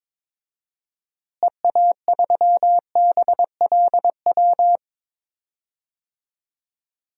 Morse code, heard as a keyed tone, about 22 wpm, 700 Hz. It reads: EA3BLW
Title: DXCC-100 Call Signs - 6 Characters - Repeated 3x - 22wpm